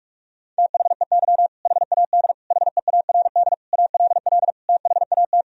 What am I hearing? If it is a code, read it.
Morse code, heard as a keyed tone, about 41 wpm, 700 Hz. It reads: THEY HAD HEARD ALL THAT
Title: Sentences from Top 400 Words and Abbreviations 45wpm